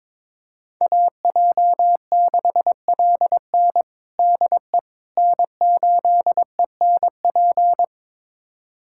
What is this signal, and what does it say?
Morse code, heard as a keyed tone, about 22 wpm, 700 Hz. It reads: AJ6LN DE N8ENP
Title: USA Callsign DE Callsign - 5 Characters - 22wpm